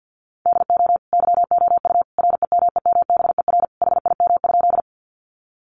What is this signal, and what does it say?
Morse code, heard as a keyed tone, about 50 wpm, 700 Hz. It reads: DO YOU REMEMBER HIM?
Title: Sentences from Top 400 Words 50wpm